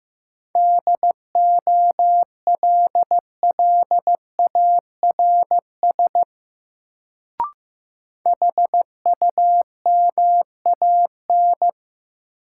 Morse code, heard as a keyed tone, about 15 wpm, 700 Hz. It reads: DOLLARS HUMAN
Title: Sets of 5 Words - New Word Drill - 15wpm